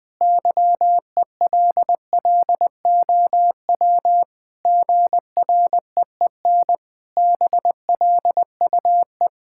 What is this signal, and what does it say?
Morse code, heard as a keyed tone, about 19 wpm, 700 Hz. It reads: YELLOW GREEN BLUE
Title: Pangram Challenge v2 - 20wpm